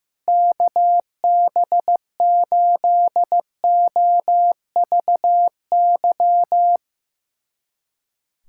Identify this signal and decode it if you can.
Morse code, heard as a keyed tone, about 15 wpm, 700 Hz. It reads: KB8OVY
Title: USA Callsign DE Callsign - 6 Characters - 15wpm